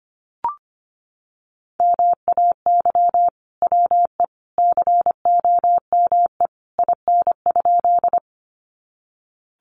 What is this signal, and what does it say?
Morse code, heard as a keyed tone, about 25 wpm, 700 Hz. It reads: MAY WE COME IN?
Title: Sentences from Top 100 Words 25wpm